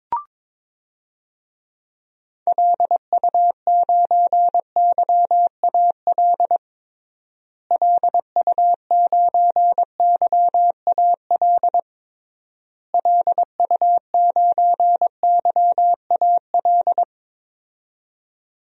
Morse code, heard as a keyed tone, about 22 wpm, 700 Hz. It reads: LU9YAL LU9YAL LU9YAL
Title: DXCC-100 Call Signs - 6 Characters - Repeated 3x - 22wpm